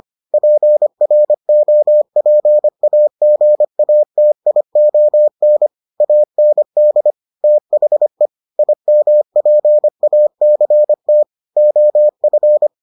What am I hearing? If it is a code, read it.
Morse code, heard as a keyed tone, about 25 wpm, 600 Hz. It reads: PROPAGATION AND THE IMPACT OF